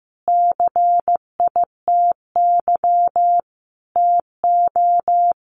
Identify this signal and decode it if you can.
Morse code, heard as a keyed tone, about 14 wpm, 700 Hz. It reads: CITY TO
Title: Sentences from Top 900 Words 15wpm